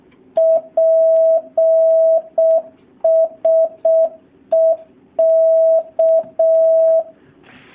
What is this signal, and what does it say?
Morse code, heard as a keyed tone, about 6 wpm, 650 Hz. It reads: PSE K
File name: jh8cbh / 2019cw / SP07- w